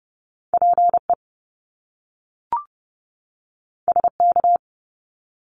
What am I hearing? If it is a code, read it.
Morse code, heard as a keyed tone, about 30 wpm, 700 Hz. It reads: PE SK
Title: Canadian Provinces and Territories 30wpm